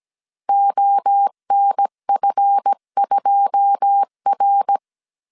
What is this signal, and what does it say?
Morse code, heard as a keyed tone, about 17 wpm, 800 Hz. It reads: ONF2R
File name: 1500C S B 16wpm